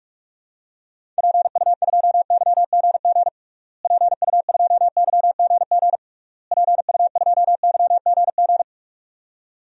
Morse code, heard as a keyed tone, about 46 wpm, 700 Hz. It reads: PU2YGG PU2YGG PU2YGG
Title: DXCC-100 Call Signs - 6 Characters - Repeated 3x - 45wpm